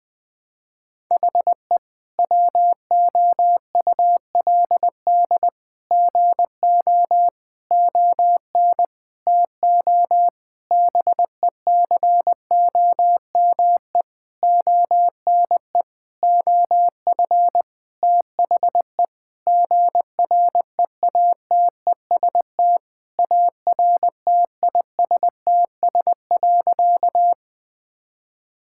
Morse code, heard as a keyed tone, about 20 wpm, 700 Hz. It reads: HE WOULD GO ON TO BECOME ONE OF THE GREATEST ARTISTS.
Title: Sentences with 8-Word Phrases - 20wpm